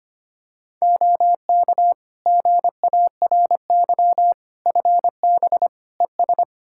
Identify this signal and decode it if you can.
Morse code, heard as a keyed tone, about 25 wpm, 700 Hz. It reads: OK GARY FB ES
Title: Practice QSO - Exchange 2 - Info - 25wpm